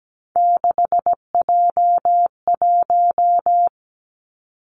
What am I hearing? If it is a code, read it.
Morse code, heard as a keyed tone, about 17 wpm, 700 Hz. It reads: 6J1
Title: Three Challenging Letter-Number Combinations 17wpm